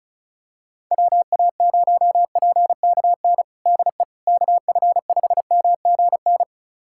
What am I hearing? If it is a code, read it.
Morse code, heard as a keyed tone, about 36 wpm, 690 Hz. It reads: WA0PKN DE KF5MGN
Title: USA Callsign DE Callsign - 6 Characters - 35wpm